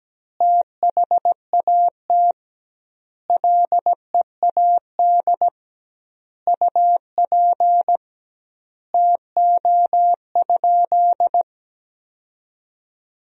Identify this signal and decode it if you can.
Morse code, heard as a keyed tone, about 17 wpm, 700 Hz. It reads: THAT LEAD UP TO?
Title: 2x Word-Spacing - Sentences from Top 500 Words - 17wpm